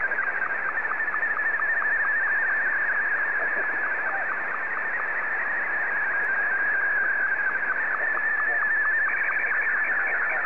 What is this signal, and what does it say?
SSTV (slow-scan television), Martin 2, with speech.